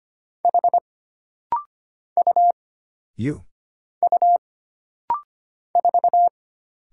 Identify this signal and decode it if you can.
Morse code, heard as a keyed tone, about 25 wpm, 700 Hz. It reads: H U U 4
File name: Single Character (Letter:Number:Punctuation:Prosign) - Rapid-Fire - 25wpm